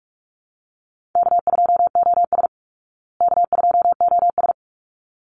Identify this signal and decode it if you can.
Morse code, heard as a keyed tone, about 44 wpm, 700 Hz. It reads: K2OS K2OS
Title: Easy Call Signs Repeated 2x - 45wpm